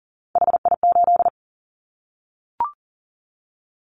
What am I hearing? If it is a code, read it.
Morse code, heard as a keyed tone, about 40 wpm, 700 Hz. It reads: HI8
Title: Callsign Prefixes for Top 100 DXCC - 40wpm